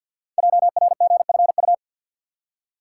Morse code, heard as a keyed tone, about 48 wpm, 700 Hz. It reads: JRGFU
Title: Code Groups - 5 Letters - 50wpm